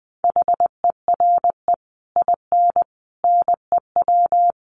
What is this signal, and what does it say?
Morse code, heard as a keyed tone, about 20 wpm, 700 Hz. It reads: HERE IN NEW
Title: Sets of 5 Words Encore2 20wpm